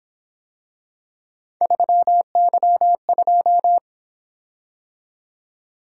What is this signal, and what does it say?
Morse code, heard as a keyed tone, about 26 wpm, 700 Hz. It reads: 3Y2
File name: Speed-Racing - Three Letter-Number Combinations - 22wpm